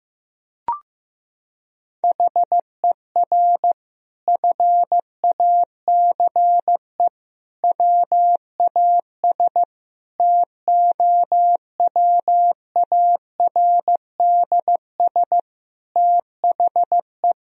Morse code, heard as a keyed tone, about 15 wpm, 700 Hz. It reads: HER FACE WAS TOWARDS THE
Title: Sentences from Top 1000 Words - Review Part 2 - 15wpm